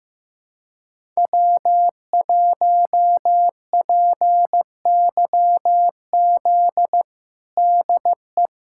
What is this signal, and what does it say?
Morse code, heard as a keyed tone, about 15 wpm, 700 Hz. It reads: W1PYZ DE
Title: USA Callsign DE Callsign - 5 Characters - 15wpm